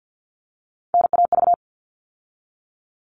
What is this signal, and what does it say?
Morse code, heard as a keyed tone, about 48 wpm, 700 Hz. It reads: NAV
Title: All Cut Numbers - 3-Digit 50wpm